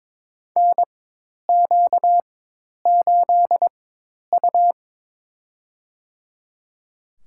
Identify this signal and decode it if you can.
Morse code, heard as a keyed tone, about 22 wpm, spaced as for 10 wpm, 700 Hz. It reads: NQ8U
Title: USA Call Signs - 4x Character Spacing - 22wpm